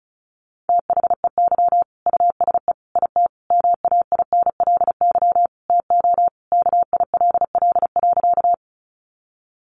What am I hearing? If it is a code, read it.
Morse code, heard as a keyed tone, about 34 wpm, 700 Hz. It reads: THEY USE IT MAINLY TO KILL.
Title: Sentences from Top 1400 Words 35wpm